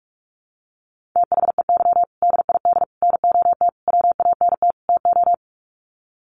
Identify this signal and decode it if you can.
Morse code, heard as a keyed tone, about 44 wpm, 700 Hz. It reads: THEY DID NOT WANT TO
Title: Sets of 5 Words 45wpm